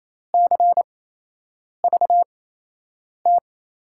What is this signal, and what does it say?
Morse code, heard as a keyed tone, about 28 wpm, spaced as for 8 wpm, 700 Hz. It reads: CVT
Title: USA Call Signs - 8x Character Spacing - 28wpm